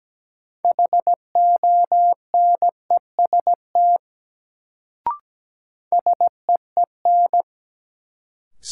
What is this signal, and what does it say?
Morse code, heard as a keyed tone, about 17 wpm, 700 Hz. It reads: HONEST SEEN
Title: Sets of 5 Words Encore - New Word Drill - 17wpm